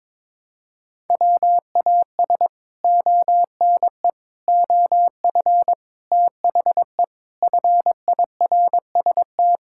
Morse code, heard as a keyed tone, about 22 wpm, 700 Hz. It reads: WAS ONE OF THE FIRST
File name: practice / 22wpm / Sets of 6 Words - 22wpm